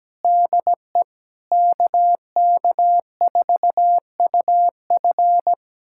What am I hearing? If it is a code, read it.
Morse code, heard as a keyed tone, about 17 wpm, 700 Hz. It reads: DE KK4UF
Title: USA Callsign DE Callsign - 5 Characters - 17wpm